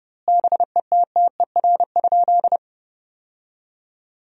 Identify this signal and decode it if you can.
Morse code, heard as a keyed tone, about 30 wpm, 700 Hz. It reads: BETTER?
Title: 2x Word-Spacing - Sentences from Top 400 Words - 30wpm